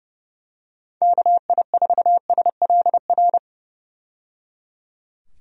Morse code, heard as a keyed tone, about 30 wpm, 700 Hz. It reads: KI4SLR
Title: USA Call Signs - 6 Characters - 30wpm